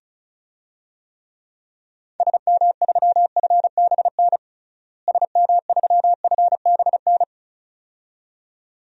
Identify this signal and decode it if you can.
Morse code, heard as a keyed tone, about 36 wpm, 700 Hz. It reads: SM3FBN SM3FBN
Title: DXCC-100 Call Signs - 6 Characters - Repeated 2x - 35wpm